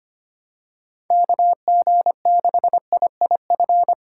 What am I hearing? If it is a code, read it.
Morse code, heard as a keyed tone, about 25 wpm, 700 Hz. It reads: KG6IIF
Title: USA Call Signs - 6 Characters - 6x Character Spacing - 25wpm